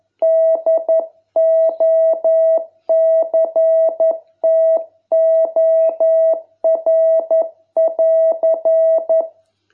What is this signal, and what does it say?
Morse code, heard as a keyed tone, about 11 wpm, 660 Hz. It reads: DOCTOR<AR>